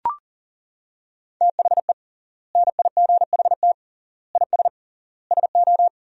Morse code, heard as a keyed tone, about 40 wpm, 700 Hz. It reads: THE NIGHT IS SO
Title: 3x Word-Spacing - Sentences from Top 500 Words - 40wpm